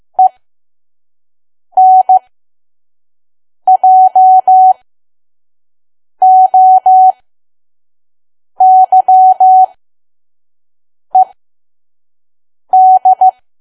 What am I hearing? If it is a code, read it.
Morse code, heard as a keyed tone, about 15 wpm, spaced as for 5 wpm, 750 Hz. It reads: ENJOYED
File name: Archive / 5 WPM files / PM